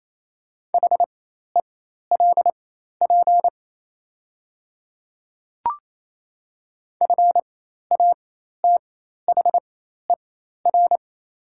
Morse code, heard as a keyed tone, about 27 wpm, spaced as for 13 wpm, 690 Hz. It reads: HELP FATHER
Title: 4x Character Spacing - Top 200 Words - 28wpm